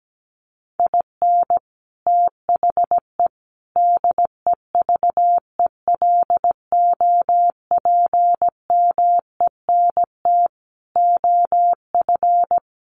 Morse code, heard as a keyed tone, about 17 wpm, 700 Hz. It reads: IN THE DEVELOPMENT OF
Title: Sets of 5 Words Encore2 17wpm